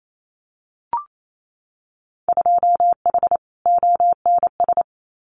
Morse code, heard as a keyed tone, about 28 wpm, 700 Hz. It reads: 2H ONS
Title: Winter Field Day Exchanges Encore 28wpm